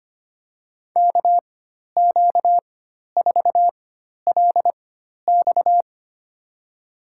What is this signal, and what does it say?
Morse code, heard as a keyed tone, about 25 wpm, spaced as for 12 wpm, 700 Hz. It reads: KQ4LX
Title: USA Call Signs - 5 Characters - 4x Character Spacing - 25wpm